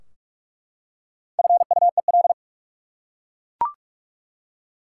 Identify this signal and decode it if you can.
Morse code, heard as a keyed tone, about 43 wpm, 700 Hz. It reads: FUEL
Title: Four Letter Words - 45wpm